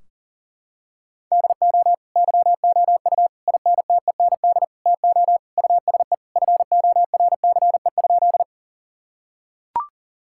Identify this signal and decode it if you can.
Morse code, heard as a keyed tone, about 37 wpm, 700 Hz. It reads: DO YOU INTEND TO USE FORCE?
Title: Sentences from Top 1600 Words 40wpm